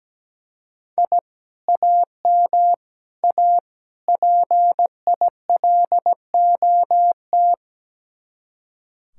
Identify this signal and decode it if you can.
Morse code, heard as a keyed tone, about 17 wpm, 700 Hz. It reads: I AM A PILOT
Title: Speed-Racing Common QSO Phrases Encore - 17wpm